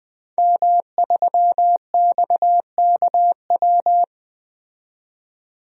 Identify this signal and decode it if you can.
Morse code, heard as a keyed tone, about 20 wpm, 700 Hz. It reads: M3XKW
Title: DXCC-100 Call Signs - 5 Characters - 20wpm